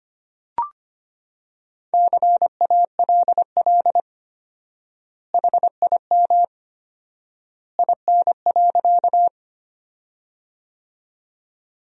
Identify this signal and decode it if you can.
Morse code, heard as a keyed tone, about 25 wpm, 700 Hz. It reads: CALL HIM IN.
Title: 4x Word-Spacing - Sentences from Top 100 Words - 25wpm